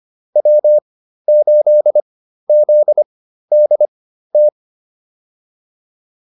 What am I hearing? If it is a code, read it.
Morse code, heard as a keyed tone, about 25 wpm, spaced as for 13 wpm, 600 Hz. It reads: W8ZDT